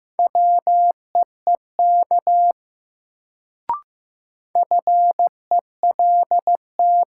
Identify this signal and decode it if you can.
Morse code, heard as a keyed tone, about 15 wpm, 700 Hz. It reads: WEEK FELT
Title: Sets of 5 Words Encore - New Word Drill - 15wpm